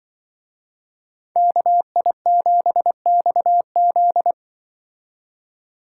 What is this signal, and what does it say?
Morse code, heard as a keyed tone, about 25 wpm, 700 Hz. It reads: KI7XZ
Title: Speed Racing - USA 2x2 Call Signs 20wpm